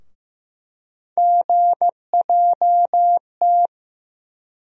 Morse code, heard as a keyed tone, about 15 wpm, 700 Hz. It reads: GJT
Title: USA Call Sign Postfixes - 3 Letters 15wpm